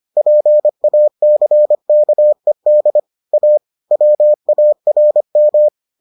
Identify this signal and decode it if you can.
Morse code, heard as a keyed tone, about 25 wpm, 600 Hz. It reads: PACKED A WARM